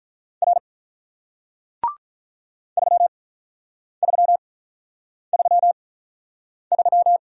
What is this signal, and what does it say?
Morse code, heard as a keyed tone, about 47 wpm, spaced as for 9 wpm, 700 Hz. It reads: R 3333